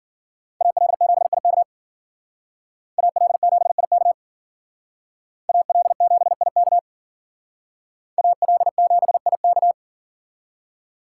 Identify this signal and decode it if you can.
Morse code, heard as a keyed tone, about 52 wpm, 700 Hz. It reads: AL7IK AL7IK AL7IK AL7IK